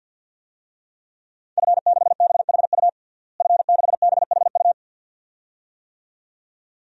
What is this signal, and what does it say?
Morse code, heard as a keyed tone, about 49 wpm, 700 Hz. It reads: F6BHU F6BHU
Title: DXCC-100 Call Signs - 5 Characters - Repeated 2x - 50wpm